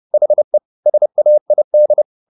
Morse code, heard as a keyed tone, about 30 wpm, 600 Hz. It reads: HE SAID
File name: the old man and the sea / the old man and the sea 1 30wpm